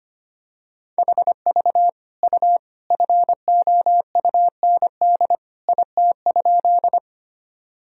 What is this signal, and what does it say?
Morse code, heard as a keyed tone, about 25 wpm, 700 Hz. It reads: HV U FOUND IT?
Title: Sentences from Top 300 Words and Abbreviations 25wpm